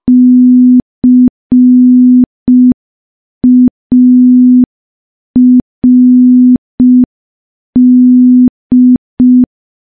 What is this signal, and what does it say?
Morse code, heard as a keyed tone, about 5 wpm, 250 Hz. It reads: CARD
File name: morse / morse-training-05WPM